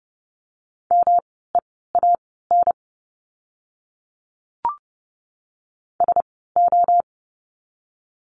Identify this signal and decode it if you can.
Morse code, heard as a keyed tone, about 30 wpm, spaced as for 17 wpm, 700 Hz. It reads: MEAN SO
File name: course / 30:10wpm / 230 Lesson 37f - Letter x - Words 30wpm